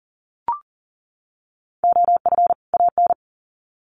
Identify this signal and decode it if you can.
Morse code, heard as a keyed tone, about 40 wpm, 700 Hz. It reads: OF AN
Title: Sets of 2 Words 40wpm